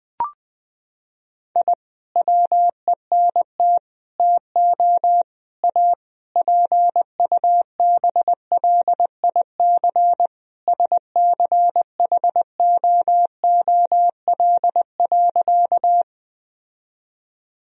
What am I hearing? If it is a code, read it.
Morse code, heard as a keyed tone, about 21 wpm, 700 Hz. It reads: I WENT TO A PUBLIC SCHOOL.
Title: Sentences from Top 1000 Words - Review Part 3 - 20wpm